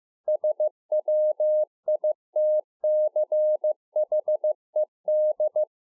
Morse code, heard as a keyed tone, about 15 wpm, 610 Hz. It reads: SWITCHED